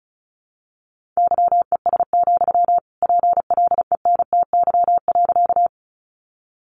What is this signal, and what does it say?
Morse code, heard as a keyed tone, about 34 wpm, 700 Hz. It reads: YES, PLENTY.
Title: Sentences from Top 1300 Words 35wpm